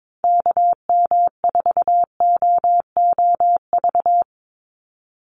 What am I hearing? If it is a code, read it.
Morse code, heard as a keyed tone, about 22 wpm, 700 Hz. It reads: KM4OOV
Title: USA Call Signs 22wpm